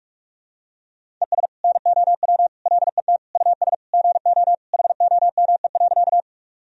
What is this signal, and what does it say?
Morse code, heard as a keyed tone, about 41 wpm, 700 Hz. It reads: ES NOW LET US GO HOME.
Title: Sentences from Top 300 Words and Abbreviations 45wpm